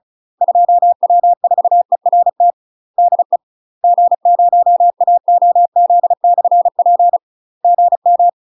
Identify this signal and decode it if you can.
Morse code, heard as a keyed tone, about 36 wpm, 700 Hz. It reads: 2W4ERT DE G0AOZ/P GM
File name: UK MASTER QSOs TRAINING / 35WPM / UK MASTER QSOs 0001